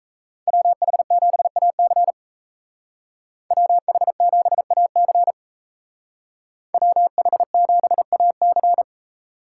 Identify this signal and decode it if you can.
Morse code, heard as a keyed tone, about 36 wpm, 700 Hz. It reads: WH7AC WH7AC WH7AC